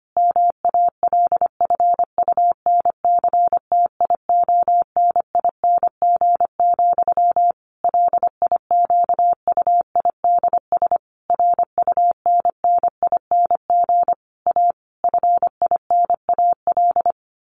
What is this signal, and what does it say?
Morse code, heard as a keyed tone, about 25 wpm, 700 Hz. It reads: MALFUNCTIONING, LIQUIDS RUNNING A FINAL
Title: A Sound Of Thunder 25wpm